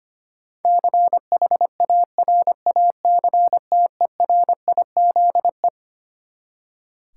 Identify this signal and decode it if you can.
Morse code, heard as a keyed tone, about 25 wpm, 700 Hz. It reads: CHARACTERIZE
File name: Twelve Letter Words - 25wpm